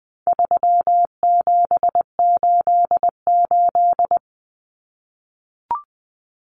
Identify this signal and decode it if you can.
Morse code, heard as a keyed tone, about 20 wpm, 700 Hz. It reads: 3788
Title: Common Cut Numbers - 4-Digit 20wpm